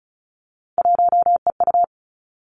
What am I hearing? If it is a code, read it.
Morse code, heard as a keyed tone, about 36 wpm, 700 Hz. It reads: 1EU
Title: Three Challenging Letter-Number Combinations 35wpm